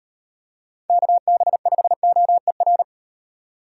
Morse code, heard as a keyed tone, about 36 wpm, 700 Hz. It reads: KB5OER